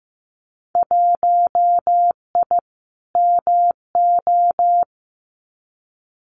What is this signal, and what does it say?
Morse code, heard as a keyed tone, about 15 wpm, 700 Hz. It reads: 1I MO
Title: Winter Field Day Exchanges Encore 15wpm